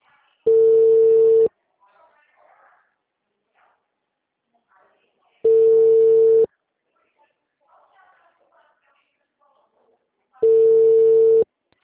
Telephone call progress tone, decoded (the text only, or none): ringback tone